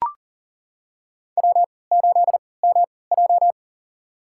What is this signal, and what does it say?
Morse code, heard as a keyed tone, about 40 wpm, spaced as for 23 wpm, 700 Hz. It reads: W8MJ
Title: USA Call Signs - 3x Character Spacing - 40wpm